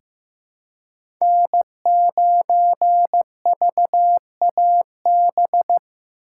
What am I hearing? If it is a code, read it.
Morse code, heard as a keyed tone, about 15 wpm, 700 Hz. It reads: N9VAB
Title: USA Call Signs - 5 Characters - 3x Character Spacing - 15wpm